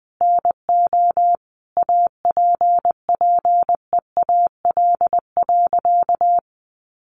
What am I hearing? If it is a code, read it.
Morse code, heard as a keyed tone, about 20 wpm, 700 Hz. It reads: NO APPEAL.
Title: Sentences from Top 1300 Words 20wpm